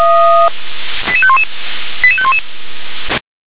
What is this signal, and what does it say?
Five-tone selective call (ZVEI), heard as quick runs of five tones, tones of about 70 ms. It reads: 88511 88511
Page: SDR/ZVEI